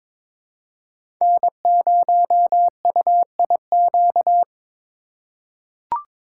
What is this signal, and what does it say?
Morse code, heard as a keyed tone, about 22 wpm, 700 Hz. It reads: N0UIQ
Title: USA Call Signs - 5 Characters - 4x Character Spacing - 22wpm